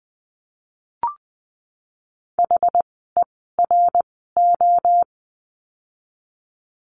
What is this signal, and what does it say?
Morse code, heard as a keyed tone, about 20 wpm, spaced as for 15 wpm, 700 Hz. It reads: HERO
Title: 061 Lesson 11f - Letter h - Words 20wpm